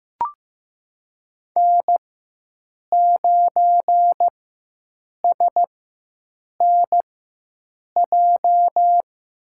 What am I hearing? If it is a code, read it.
Morse code, heard as a keyed tone, about 15 wpm, spaced as for 7 wpm, 700 Hz. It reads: N9SNJ
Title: USA Call Signs - 5 Characters - 4x Character Spacing - 15wpm